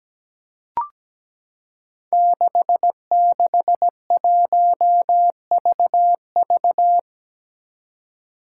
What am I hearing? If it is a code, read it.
Morse code, heard as a keyed tone, about 17 wpm, 700 Hz. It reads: 661VV
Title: Five Challenging Letter-Number Combinations 17wpm